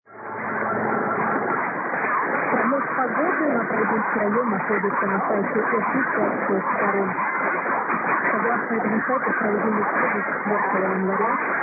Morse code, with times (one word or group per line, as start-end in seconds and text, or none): none